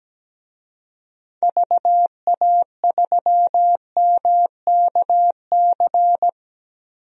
Morse code, heard as a keyed tone, about 17 wpm, 700 Hz. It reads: VA3MKC
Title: DXCC-100 Call Signs - 6 Characters - Repeated 3x - 17wpm